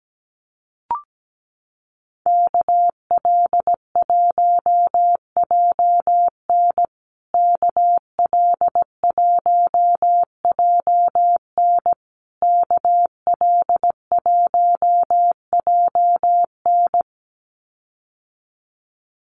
Morse code, heard as a keyed tone, about 17 wpm, 700 Hz. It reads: KL1JN KL1JN KL1JN
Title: USA Call Signs Repeated 3x 17wpm